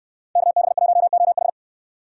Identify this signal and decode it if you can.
Morse code, heard as a keyed tone, about 69 wpm, 700 Hz. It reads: KB1OH